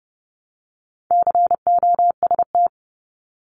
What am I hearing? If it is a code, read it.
Morse code, heard as a keyed tone, about 30 wpm, 700 Hz. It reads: COST
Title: Multi-letter Phonemes 30wpm